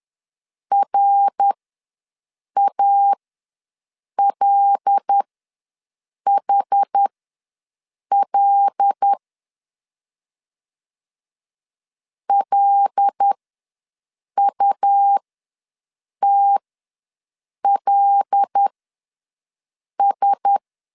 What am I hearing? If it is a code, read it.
Morse code, heard as a keyed tone, about 10 wpm, spaced as for 6 wpm, 800 Hz. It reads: RALHL LUTLS
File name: LEZIONI / 5 LBVC / 5 L ws 10wpm 3636